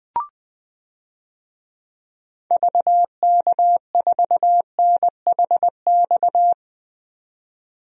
Morse code, heard as a keyed tone, about 20 wpm, 700 Hz. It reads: VK4NHX
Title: DXCC-100 Call Signs - 6 Characters - 20wpm